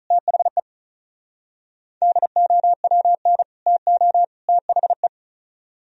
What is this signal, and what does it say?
Morse code, heard as a keyed tone, about 43 wpm, 700 Hz. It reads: THE DOWN TO THE
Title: Speed-Racing - Sets of 3 Words 35wpm